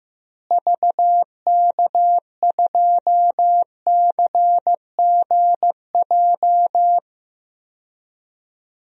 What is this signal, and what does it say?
Morse code, heard as a keyed tone, about 15 wpm, 700 Hz. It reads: VK2CGJ
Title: DXCC-100 Call Signs - 6 Characters - Repeated 3x - 15wpm